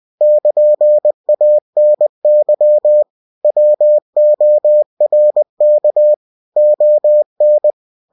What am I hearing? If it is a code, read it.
Morse code, heard as a keyed tone, about 20 wpm, 600 Hz. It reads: <KN>ANY WORK ON
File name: cw-books / winnie-the-pooh / winnie-20wpm-page-0115